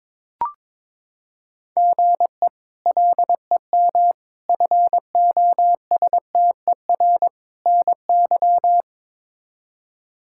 Morse code, heard as a keyed tone, about 21 wpm, 700 Hz. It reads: GE LEM FOSTER NY
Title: K1USN SST QSO Practice - 22wpm